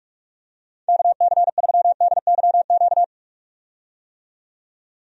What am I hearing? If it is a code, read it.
Morse code, heard as a keyed tone, about 46 wpm, 700 Hz. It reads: KC3DYQ
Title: USA Call Signs - 6 Characters - 45wpm